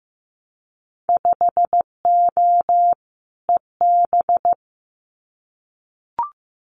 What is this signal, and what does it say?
Morse code, heard as a keyed tone, about 15 wpm, 700 Hz. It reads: 5O EB
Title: Winter Field Day Exchanges Encore 15wpm